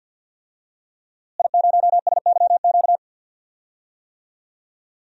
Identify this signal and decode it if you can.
Morse code, heard as a keyed tone, about 53 wpm, 700 Hz. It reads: I0SYQ
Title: DXCC-100 Call Signs - 5 Characters - Repeated 3x - 50wpm